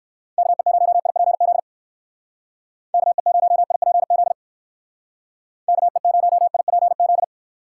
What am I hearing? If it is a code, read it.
Morse code, heard as a keyed tone, about 59 wpm, 700 Hz. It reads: KE0IPZ KE0IPZ KE0IPZ